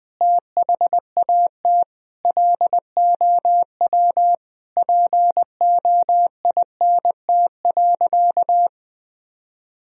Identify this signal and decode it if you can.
Morse code, heard as a keyed tone, about 21 wpm, 700 Hz. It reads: THAT LOW POINT.
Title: Sentences from Top 1000 Words - Review Part 1 - 20wpm